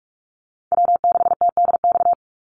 Morse code, heard as a keyed tone, about 44 wpm, 700 Hz. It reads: W6TDX
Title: NA SOTA Chaser Callsigns 45wpm